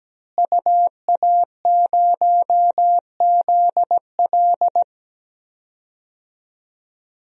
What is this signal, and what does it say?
Morse code, heard as a keyed tone, about 17 wpm, 700 Hz. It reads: UA0ZL
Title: DXCC-100 Call Signs - 5 Characters - Repeated 3x - 17wpm